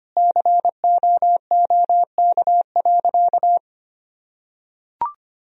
Morse code, heard as a keyed tone, about 25 wpm, 700 Hz. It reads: COOK.
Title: Sentences from Top 1000 Words - Review Part 4 - 25wpm